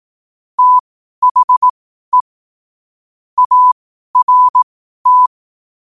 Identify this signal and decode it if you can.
Morse code, heard as a keyed tone, about 18 wpm, spaced as for 12 wpm, 1000 Hz. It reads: THE ART
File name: qotdfiles / 12wpm